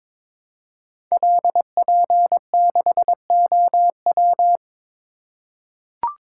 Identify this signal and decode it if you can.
Morse code, heard as a keyed tone, about 22 wpm, 700 Hz. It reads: LP6OW
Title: Code Groups - 5 Letter:Numbers - 22wpm